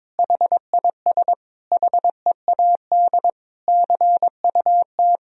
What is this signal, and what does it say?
Morse code, heard as a keyed tone, about 22 wpm, 700 Hz. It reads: HIS HEAD CUT
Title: Sentences from Top 1000 Words - Review Part 1 - 22wpm